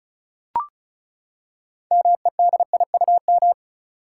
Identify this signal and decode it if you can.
Morse code, heard as a keyed tone, about 36 wpm, 700 Hz. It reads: MEDIUM
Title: Six Letter Words - 35wpm